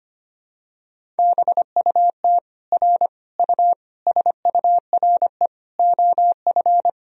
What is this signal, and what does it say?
Morse code, heard as a keyed tone, about 25 wpm, 700 Hz. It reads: BUT R U SURE OF
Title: Sentences from Top 300 Words and Abbreviations 25wpm